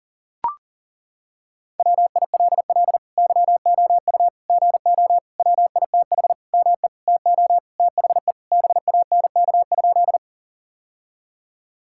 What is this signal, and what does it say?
Morse code, heard as a keyed tone, about 40 wpm, 700 Hz. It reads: WILL YOU GO WITH ME TO THE BANK?
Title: Sentences from Top 1000 Words - Review Part 6 - 40wpm